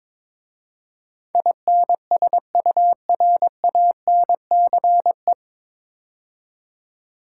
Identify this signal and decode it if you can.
Morse code, heard as a keyed tone, about 22 wpm, 690 Hz. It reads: INSURANCE
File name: 9 Letter Words - First 3 Letters Given - 22wpm